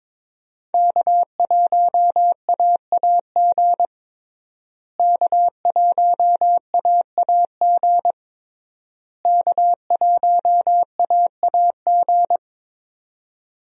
Morse code, heard as a keyed tone, about 22 wpm, 700 Hz. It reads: K1AAG K1AAG K1AAG
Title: USA Call Signs - 5 Characters - Repeated 3x - 22wpm